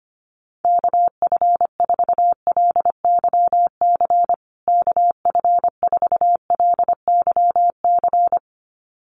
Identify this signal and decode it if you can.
Morse code, heard as a keyed tone, about 25 wpm, 700 Hz. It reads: KF4LYC KF4LYC
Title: USA Call Signs Repeated 2x 25wpm